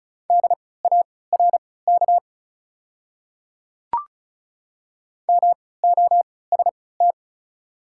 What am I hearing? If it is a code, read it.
Morse code, heard as a keyed tone, about 36 wpm, spaced as for 20 wpm, 700 Hz. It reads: DARK MOST
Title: 3x Character Spacing - Top 500 Words - 35wpm